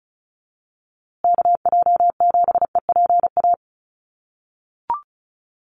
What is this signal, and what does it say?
Morse code, heard as a keyed tone, about 34 wpm, 700 Hz. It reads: KJ7EPA
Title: USA Call Signs Repeated 2x 35wpm